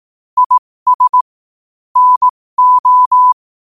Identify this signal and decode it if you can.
Morse code, heard as a keyed tone, about 18 wpm, spaced as for 15 wpm, 1000 Hz. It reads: IS NO
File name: qotdfiles / 15wpm